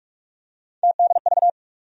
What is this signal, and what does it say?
Morse code, heard as a keyed tone, about 42 wpm, 700 Hz. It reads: TDV
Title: USA Call Sign Postfixes - 3 Letters 45wpm